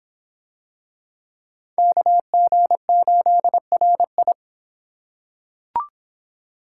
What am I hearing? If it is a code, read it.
Morse code, heard as a keyed tone, about 26 wpm, 700 Hz. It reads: KG8RI